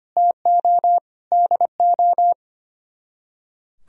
Morse code, heard as a keyed tone, about 32 wpm, 700 Hz. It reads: TO DO